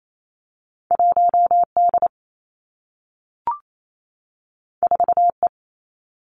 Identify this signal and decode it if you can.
Morse code, heard as a keyed tone, about 28 wpm, 700 Hz. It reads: 1 D 4 E
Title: Two Challenging Letter-Number Combinations 28wpm